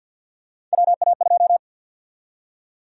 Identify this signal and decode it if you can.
Morse code, heard as a keyed tone, about 49 wpm, 700 Hz. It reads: WA2
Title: USA Call Sign Prefixes - 50wpm